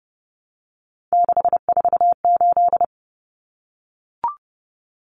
Morse code, heard as a keyed tone, about 30 wpm, 700 Hz. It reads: 648
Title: Numbers - 3-Digit 30wpm